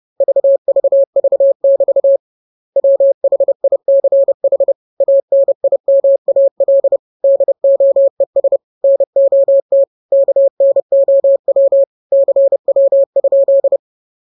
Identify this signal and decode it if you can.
Morse code, heard as a keyed tone, about 30 wpm, 550 Hz. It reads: VVV<BT> WHICH ANIMAL DOES NOT KNOW CW?
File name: jokes / joke 26 30 0000